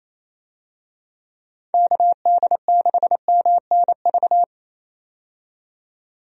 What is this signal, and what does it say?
Morse code, heard as a keyed tone, about 28 wpm, 700 Hz. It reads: KD6MNV
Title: USA Call Signs - 6 Characters - 28wpm